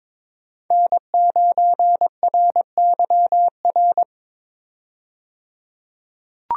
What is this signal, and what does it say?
Morse code, heard as a keyed tone, about 22 wpm, 700 Hz. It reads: N9RYR
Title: USA Call Signs - 5 Characters - 22wpm